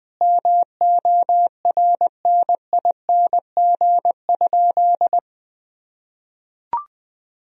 Morse code, heard as a keyed tone, about 20 wpm, 700 Hz. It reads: MORNING?
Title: Sentences from Top 1000 Words - Review Part 3 - 20wpm